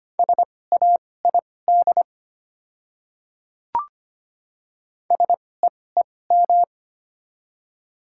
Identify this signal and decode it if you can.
Morse code, heard as a keyed tone, about 25 wpm, spaced as for 18 wpm, 700 Hz. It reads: SAID SEEM
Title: 2x Character Spacing - Top 500 Words - 25wpm